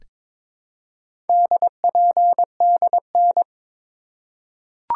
Morse code, heard as a keyed tone, about 22 wpm, 700 Hz. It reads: DPDN
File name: practice / 22wpm / Four Letter-Number Combinations - 22wpm